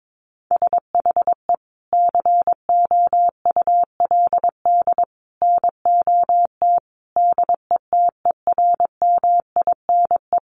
Morse code, heard as a keyed tone, about 22 wpm, 700 Hz. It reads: SHE COULD NOT DETERMINE
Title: Sentences from Top 700 Words 22wpm